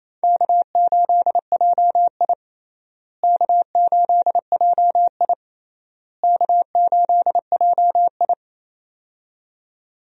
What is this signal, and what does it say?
Morse code, heard as a keyed tone, about 28 wpm, 700 Hz. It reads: K8JI K8JI K8JI
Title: USA Call Signs - 4 Characters - Repeated 3x - 28wpm